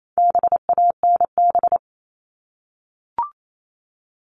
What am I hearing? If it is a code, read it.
Morse code, heard as a keyed tone, about 28 wpm, 700 Hz. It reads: BANB
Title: All Cut Numbers - 4-Digit 28wpm